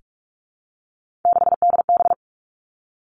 Morse code, heard as a keyed tone, about 46 wpm, 700 Hz. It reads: 6DB
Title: Three Challenging Letter-Number Combinations 45wpm